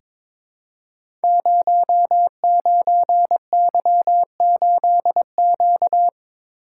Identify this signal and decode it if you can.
Morse code, heard as a keyed tone, about 23 wpm, 700 Hz. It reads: 09Y8Q
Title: Code Groups - 5 Letter:Numbers - 22wpm